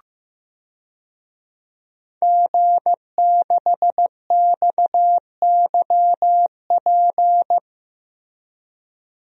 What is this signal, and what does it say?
Morse code, heard as a keyed tone, about 15 wpm, 700 Hz. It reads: G6XYP
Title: DXCC-100 Call Signs - 5 Characters - Repeated 3x - 15wpm